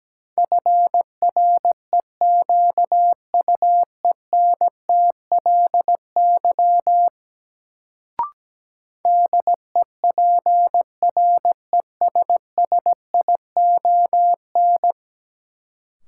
Morse code, heard as a keyed tone, about 17 wpm, 700 Hz. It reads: FREQUENTLY DEPRESSION
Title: Sentences from Top 1300 Words - New Word Drill Alt - 17wpm